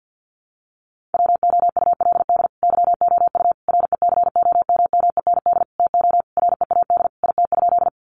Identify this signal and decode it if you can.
Morse code, heard as a keyed tone, about 50 wpm, 700 Hz. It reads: WOULD YOU RECOMMEND TO READ IT?
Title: Sentences from Top 1200 Words 50wpm